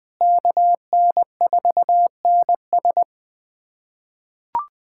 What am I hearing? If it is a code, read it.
Morse code, heard as a keyed tone, about 20 wpm, 700 Hz. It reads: KN4NS
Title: USA Callsign DE Callsign - 5 Characters - 20wpm